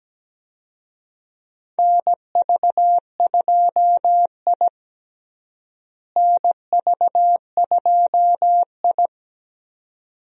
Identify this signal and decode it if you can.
Morse code, heard as a keyed tone, about 17 wpm, 700 Hz. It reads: NV2I NV2I
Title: USA Call Signs - 4 Characters - Repeated 2x - 17wpm